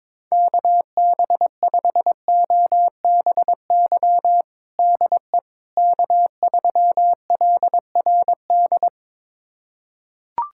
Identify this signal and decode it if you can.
Morse code, heard as a keyed tone, about 22 wpm, 700 Hz. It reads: KB5OBY DE K3LRD
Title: USA Callsign DE Callsign 22wpm